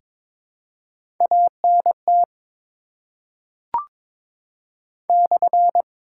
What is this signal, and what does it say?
Morse code, heard as a keyed tone, about 21 wpm, 700 Hz. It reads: ANT /
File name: Top 90 QSO Elements - 8x Character Spacing - 22wpm